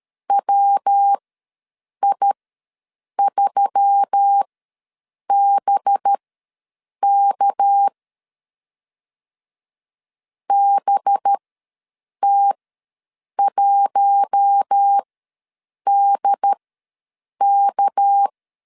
Morse code, heard as a keyed tone, about 13 wpm, spaced as for 7 wpm, 800 Hz. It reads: WI3BK BT1DK
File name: LEZIONI / 6 PKX0 / 6 PK ws 12wpm 3116